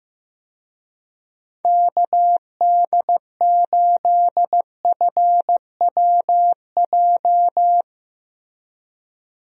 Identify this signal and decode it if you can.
Morse code, heard as a keyed tone, about 16 wpm, 700 Hz. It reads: KD8FWJ
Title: USA Call Signs - 6 Characters - Repeated 3x - 15wpm